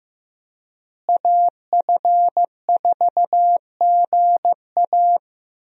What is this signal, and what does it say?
Morse code, heard as a keyed tone, about 15 wpm, 700 Hz. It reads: AF4GA
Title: USA Call Signs - 5 Characters - Repeated 2x - 15wpm